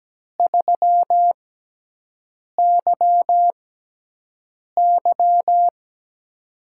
Morse code, heard as a keyed tone, about 17 wpm, spaced as for 6 wpm, 700 Hz. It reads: 3YY
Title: USA Call Signs - 6x Character Spacing - 17wpm